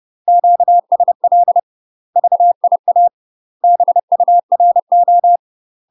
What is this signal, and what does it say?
Morse code, heard as a keyed tone, about 30 wpm, 700 Hz. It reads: QSL VIA BURO
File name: UK MASTER QSOs TRAINING / 30WPM / UK MASTER QSOs 0028